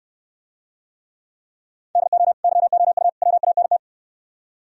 Morse code, heard as a keyed tone, about 67 wpm, 700 Hz. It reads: DO YOU WANT